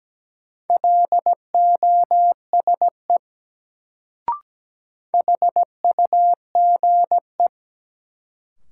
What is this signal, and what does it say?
Morse code, heard as a keyed tone, about 17 wpm, 700 Hz. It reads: LOSE HUGE
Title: Sentences from Top 1100 Words - New Word Drill Alt - 17wpm